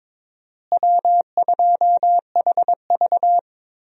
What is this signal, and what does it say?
Morse code, heard as a keyed tone, about 22 wpm, 700 Hz. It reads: W2HV
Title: USA Call Signs - 4 Characters - Repeated 3x - 22wpm